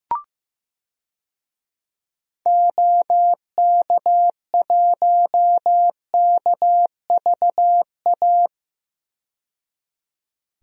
Morse code, heard as a keyed tone, about 15 wpm, 700 Hz. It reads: OK1KVA
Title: DXCC-100 Call Signs - 6 Characters - 15wpm